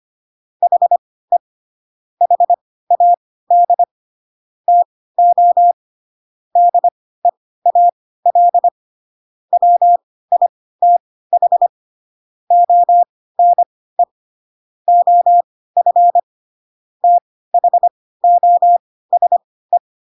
Morse code, heard as a keyed tone, about 25 wpm, spaced as for 16 wpm, 700 Hz. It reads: HE HAD TO DEAL WITH ONE OF THOSE
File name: Pinocchio / Pinocchio ch6 10 25 ik1hge